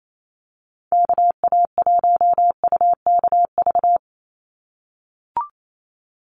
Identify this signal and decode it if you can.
Morse code, heard as a keyed tone, about 28 wpm, 700 Hz. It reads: KA1UKV
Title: USA Call Signs 28wpm